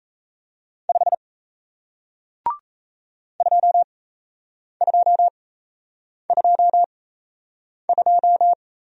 Morse code, heard as a keyed tone, about 38 wpm, spaced as for 8 wpm, 700 Hz. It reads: 5 2222